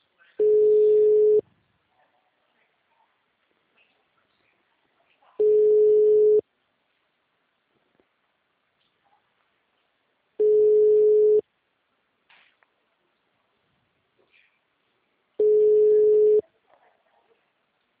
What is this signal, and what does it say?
Ringback tone (Europe, 425 Hz).